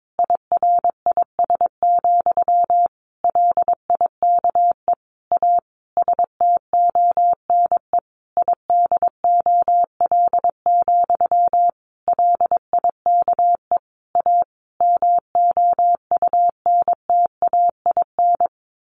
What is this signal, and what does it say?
Morse code, heard as a keyed tone, about 22 wpm, 700 Hz. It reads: IRIS, LIKE A STONE IDOL, LIKE A MOUNTAIN
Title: A Sound Of Thunder 22wpm